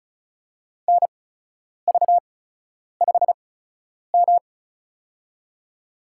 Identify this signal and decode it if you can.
Morse code, heard as a keyed tone, about 36 wpm, spaced as for 10 wpm, 700 Hz. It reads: NV5M
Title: USA Call Signs - 4 Characters - 8x Character Spacing - 35wpm